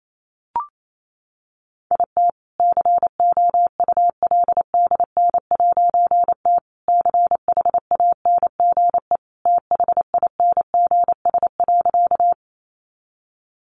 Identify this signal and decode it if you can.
Morse code, heard as a keyed tone, about 28 wpm, 700 Hz. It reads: IT COULDN'T CHANGE THINGS.
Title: A Sound Of Thunder 28wpm